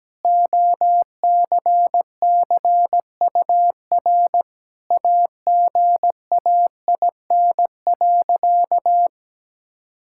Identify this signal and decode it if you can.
Morse code, heard as a keyed tone, about 17 wpm, 700 Hz. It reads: OCCUR AGAIN.
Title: Sentences from Top 1000 Words - Review Part 6 - 17wpm